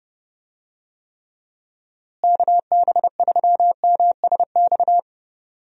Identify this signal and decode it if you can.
Morse code, heard as a keyed tone, about 29 wpm, 700 Hz. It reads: KB3MSX
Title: USA Call Signs - 6 Characters - Repeated 3x - 30wpm